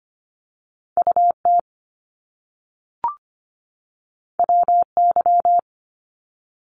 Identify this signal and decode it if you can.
Morse code, heard as a keyed tone, about 25 wpm, 700 Hz. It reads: UT WY
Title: ARRL Sections 25wpm